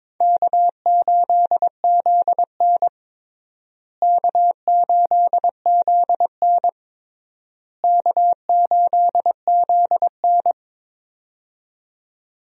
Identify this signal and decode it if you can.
Morse code, heard as a keyed tone, about 22 wpm, 700 Hz. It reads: K8ZN K8ZN K8ZN
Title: USA Call Signs - 4 Characters - Repeated 3x - 22wpm